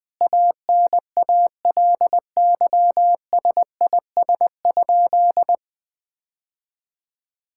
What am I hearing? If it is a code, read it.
Morse code, heard as a keyed tone, about 20 wpm, 700 Hz. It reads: ANALYSIS?
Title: Sentences from Top 1000 Words - Review Part 5 - 20wpm